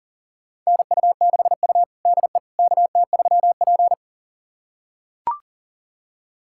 Morse code, heard as a keyed tone, about 40 wpm, 700 Hz. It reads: NU6U DE KT3P
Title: USA Callsign DE Callsign - 4 Characters - 40wpm